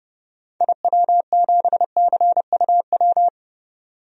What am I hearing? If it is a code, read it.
Morse code, heard as a keyed tone, about 30 wpm, 700 Hz. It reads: IW7CUW
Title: DXCC-100 Call Signs - 6 Characters - 30wpm